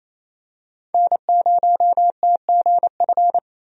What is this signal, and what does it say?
Morse code, heard as a keyed tone, about 28 wpm, 700 Hz. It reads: N0TGF
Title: USA Call Signs - 5 Characters - 28wpm